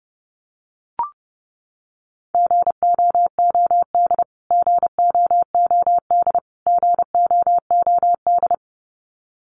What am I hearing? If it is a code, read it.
Morse code, heard as a keyed tone, about 30 wpm, 700 Hz. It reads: GOOD GOOD GOOD
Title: Top 200 Words Repeated 30wpm